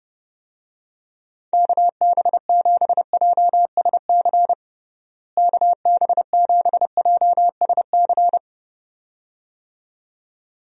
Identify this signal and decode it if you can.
Morse code, heard as a keyed tone, about 32 wpm, 700 Hz. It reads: KB7JSC KB7JSC
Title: USA Call Signs - 6 Characters - Repeated 2x - 30wpm